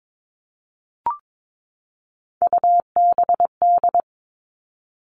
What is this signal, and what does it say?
Morse code, heard as a keyed tone, about 22 wpm, 700 Hz. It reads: UBD
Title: All Cut Numbers - 3-Digit 22wpm